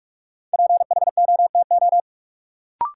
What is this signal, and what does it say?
Morse code, heard as a keyed tone, about 46 wpm, 700 Hz. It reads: PHOTO
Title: Sentences from Top 1200 Words - New Word Drill - 45wpm